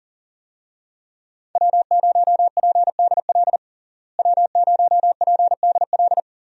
Morse code, heard as a keyed tone, about 40 wpm, 700 Hz. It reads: W0PDL W0PDL
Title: USA Call Signs - 5 Characters - Repeated 3x - 40wpm